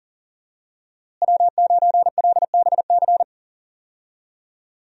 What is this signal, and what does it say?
Morse code, heard as a keyed tone, about 37 wpm, 700 Hz. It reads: W9LBC
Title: USA Call Signs - 5 Characters - Repeated 3x - 40wpm